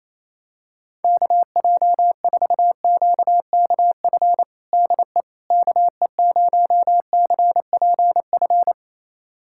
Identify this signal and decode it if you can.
Morse code, heard as a keyed tone, about 28 wpm, 700 Hz. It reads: KJ4QKF DE KE0CPF
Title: USA Callsign DE Callsign - 6 Characters - 28wpm